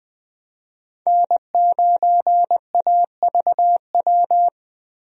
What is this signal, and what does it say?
Morse code, heard as a keyed tone, about 20 wpm, 700 Hz. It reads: N9AVW
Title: USA Call Signs - 5 Characters - 20wpm